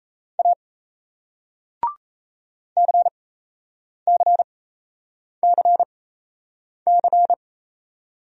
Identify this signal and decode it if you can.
Morse code, heard as a keyed tone, about 38 wpm, spaced as for 8 wpm, 700 Hz. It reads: A CCCC